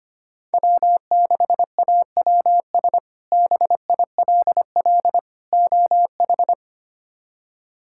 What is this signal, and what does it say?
Morse code, heard as a keyed tone, about 25 wpm, 700 Hz. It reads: W6AWS BILL OH
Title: K1USN SST QSO Practice - 25wpm